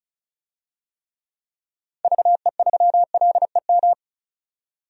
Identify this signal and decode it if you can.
Morse code, heard as a keyed tone, about 34 wpm, 700 Hz. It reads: VE3LEM
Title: DXCC-100 Call Signs - 6 Characters - 35wpm